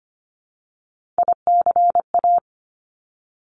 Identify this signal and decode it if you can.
Morse code, heard as a keyed tone, about 25 wpm, 700 Hz. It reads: ICA
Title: Top Three Letter Combinations in Words 25wpm